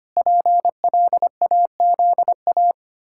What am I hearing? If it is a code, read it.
Morse code, heard as a keyed tone, about 25 wpm, 700 Hz. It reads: PLAZA
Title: Pangram Challenge v2 - 25wpm